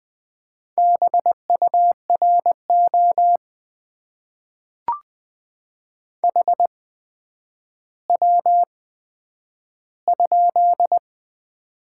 Morse code, heard as a keyed tone, about 20 wpm, spaced as for 10 wpm, 700 Hz. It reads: BURO HW?
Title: Top 90 QSO Elements - 8x Character Spacing - 20wpm